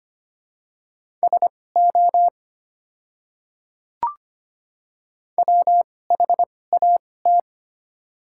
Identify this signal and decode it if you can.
Morse code, heard as a keyed tone, about 25 wpm, spaced as for 18 wpm, 700 Hz. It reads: SO WHAT
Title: 2x Character Spacing - Top 200 Words - 25wpm